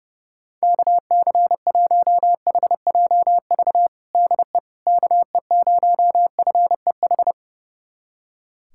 Morse code, heard as a keyed tone, about 30 wpm, 700 Hz. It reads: KC1HJV DE KE0FEH
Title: USA Callsign DE Callsign 30wpm